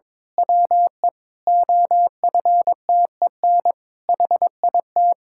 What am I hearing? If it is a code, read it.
Morse code, heard as a keyed tone, about 22 wpm, 700 Hz. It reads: WE OFTEN HIT
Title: Pangram Challenge v2 - 22wpm